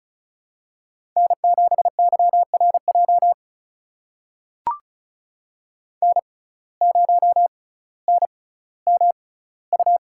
Morse code, heard as a keyed tone, about 34 wpm, spaced as for 18 wpm, 700 Hz. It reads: N7YRJ N0NMU
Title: USA Call Signs - 5 Characters - 6x Character Spacing - 35wpm